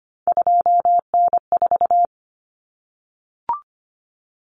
Morse code, heard as a keyed tone, about 25 wpm, 700 Hz. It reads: 2N4
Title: Common Cut Numbers - 3-Digit 25wpm